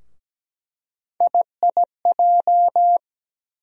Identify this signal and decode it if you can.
Morse code, heard as a keyed tone, about 17 wpm, 700 Hz. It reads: IIJ
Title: USA Call Sign Postfixes - 3 Letters 17wpm